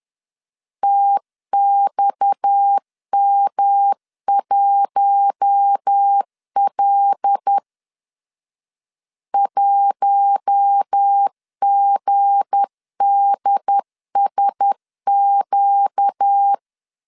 Morse code, heard as a keyed tone, about 10 wpm, 800 Hz. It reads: TXM1L 1GDSQ
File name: LEZIONI / 7 GF9Z / 7 GF9Z 10wpm 2611